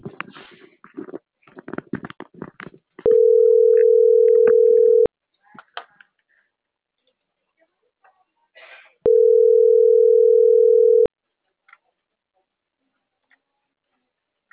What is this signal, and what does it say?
Ringback tone (North America), 440 + 480 Hz.